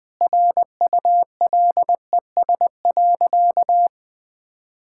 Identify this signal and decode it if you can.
Morse code, heard as a keyed tone, about 21 wpm, 700 Hz. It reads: RULES.
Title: Sentences with 2-Word Phrases - 20wpm